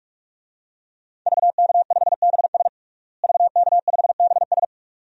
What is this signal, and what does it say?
Morse code, heard as a keyed tone, about 41 wpm, 700 Hz. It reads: VK5BS VK5BS
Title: DXCC-100 Call Signs - 5 Characters - Repeated 2x - 45wpm